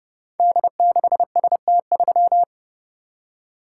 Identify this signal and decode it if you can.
Morse code, heard as a keyed tone, about 30 wpm, 700 Hz. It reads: D6ST3
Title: Code Groups - 5 Letter:Numbers - 30wpm